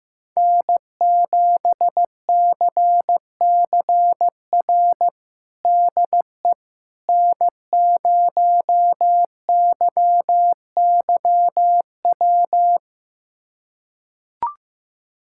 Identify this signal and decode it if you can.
Morse code, heard as a keyed tone, about 15 wpm, 700 Hz. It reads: N7CCR DE N0YYW
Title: USA Callsign DE Callsign - 5 Characters - 15wpm